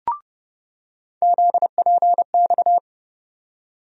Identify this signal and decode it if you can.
Morse code, heard as a keyed tone, about 29 wpm, 700 Hz. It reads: ZPX
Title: USA Call Sign Postfixes - 3 Letters 30wpm